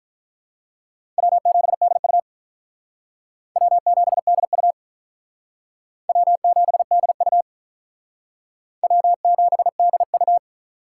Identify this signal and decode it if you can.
Morse code, heard as a keyed tone, about 46 wpm, 700 Hz. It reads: W7DU W7DU W7DU W7DU